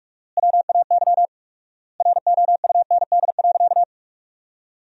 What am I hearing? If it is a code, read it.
Morse code, heard as a keyed tone, about 43 wpm, 700 Hz. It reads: WAY ROUND.
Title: 4x Word-Spacing - Sentences from Top 200 Words - 45wpm